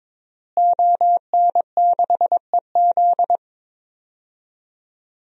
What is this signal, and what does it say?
Morse code, heard as a keyed tone, about 22 wpm, 700 Hz. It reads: ON6EZ
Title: DXCC-100 Call Signs - 5 Characters - Repeated 3x - 22wpm